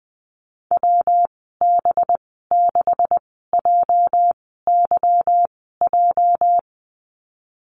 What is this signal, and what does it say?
Morse code, heard as a keyed tone, about 20 wpm, spaced as for 15 wpm, 700 Hz. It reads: WB6JYJ
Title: 219 Lesson 35g - Number 0 - Call Signs 20wpm